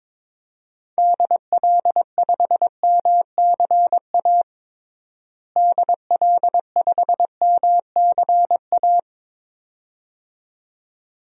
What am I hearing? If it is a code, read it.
Morse code, heard as a keyed tone, about 22 wpm, 700 Hz. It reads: DL5MCA DL5MCA
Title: DXCC-100 Call Signs - 6 Characters - Repeated 2x - 22wpm